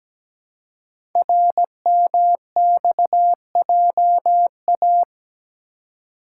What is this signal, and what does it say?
Morse code, heard as a keyed tone, about 17 wpm, 700 Hz. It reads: RMXJA
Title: Code Groups - 5 Letters - 17wpm